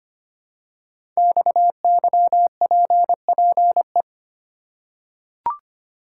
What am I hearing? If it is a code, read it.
Morse code, heard as a keyed tone, about 25 wpm, 700 Hz. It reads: XYPPE
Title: Code Groups - 5 Letters - 25wpm